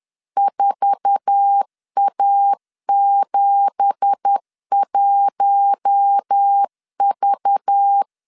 Morse code, heard as a keyed tone, about 10 wpm, 800 Hz. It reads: 4A71V
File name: CW / DET / 1 9 1500C S A 10wpm 4511